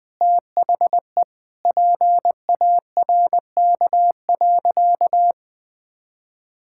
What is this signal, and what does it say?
Morse code, heard as a keyed tone, about 20 wpm, 700 Hz. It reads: THE PARK.
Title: Sentences with 4-Word Phrases - 20wpm